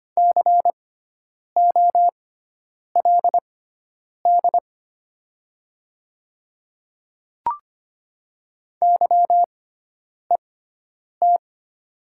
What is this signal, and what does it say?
Morse code, heard as a keyed tone, about 25 wpm, spaced as for 9 wpm, 700 Hz. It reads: COLD YET
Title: 6x Character Spacing - Top 500 Words - 25wpm